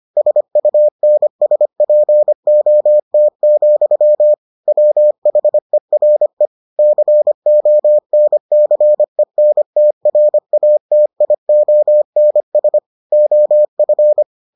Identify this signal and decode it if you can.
Morse code, heard as a keyed tone, about 25 wpm, 600 Hz. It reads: SUNSPOT, WHERE CONCENTRATIONS OF